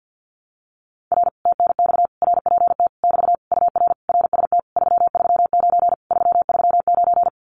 Morse code, heard as a keyed tone, about 50 wpm, 700 Hz. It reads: R TNX RPT <BT> UR RST 339 339
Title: Common QSO Phrases Encore 50wpm